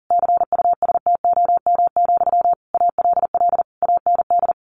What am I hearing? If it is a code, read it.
Morse code, heard as a keyed tone, about 40 wpm, 700 Hz. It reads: CUSTOM, ALL AND
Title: A Sound Of Thunder 40wpm